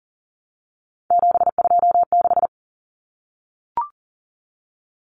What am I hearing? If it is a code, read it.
Morse code, heard as a keyed tone, about 40 wpm, 700 Hz. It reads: 726
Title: Numbers - 3-Digit 40wpm